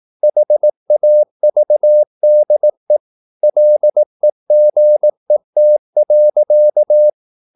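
Morse code, heard as a keyed tone, about 18 wpm, 600 Hz. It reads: HAVDE LEGET.
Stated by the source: Slow (QRS) Morse practice at 18 wpm, CW med Gnister.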